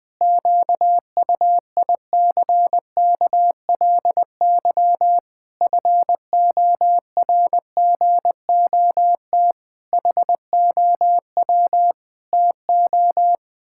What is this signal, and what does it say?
Morse code, heard as a keyed tone, about 21 wpm, 700 Hz. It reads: QUICKLY FORGOT HOW TO
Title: Pangram Challenge v2 - 20wpm